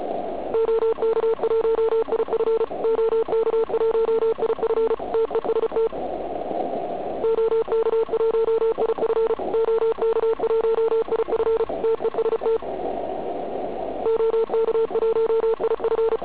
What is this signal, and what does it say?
Morse code, heard as a keyed tone, about 34 wpm, 450 Hz, with noise. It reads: OK1IF OK1IF TEST OK1IF OK1IF TEST OK1IF
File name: Archiv / CUC106 / cuc106 PILEUP